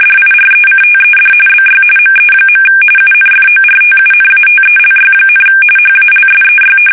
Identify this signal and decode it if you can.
RTTY, about 45.45 baud. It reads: DIRECTED BY ME. UNFORTUNATELY, HALSEY
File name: fsk850 / battleshipnj